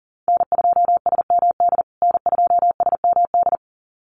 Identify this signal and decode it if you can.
Morse code, heard as a keyed tone, about 40 wpm, 700 Hz. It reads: N2SMD N2SMD
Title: USA Call Signs Repeated 2x 40wpm